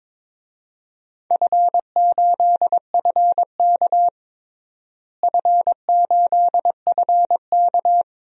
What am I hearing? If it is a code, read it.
Morse code, heard as a keyed tone, about 22 wpm, 700 Hz. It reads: F8FK F8FK
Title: DXCC-100 Call Signs - 4 Characters - Repeated 2x - 22wpm